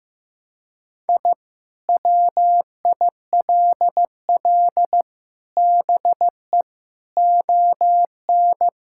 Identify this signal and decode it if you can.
Morse code, heard as a keyed tone, about 15 wpm, 700 Hz. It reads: I WILL BE ON
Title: Sentences from Top 1000 Words - Review Part 5 - 15wpm